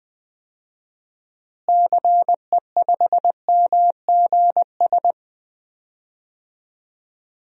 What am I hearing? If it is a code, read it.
Morse code, heard as a keyed tone, about 20 wpm, 700 Hz. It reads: CE5MGS
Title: DXCC-100 Call Signs - 6 Characters - Repeated 3x - 20wpm